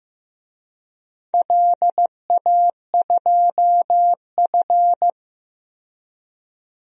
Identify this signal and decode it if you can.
Morse code, heard as a keyed tone, about 15 wpm, 700 Hz. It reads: LA2F
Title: DXCC-100 Call Signs - 4 Characters - 15wpm